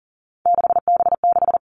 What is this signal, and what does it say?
Morse code, heard as a keyed tone, about 40 wpm, 700 Hz. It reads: 6B6
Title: All Cut Numbers - 3-Digit 40wpm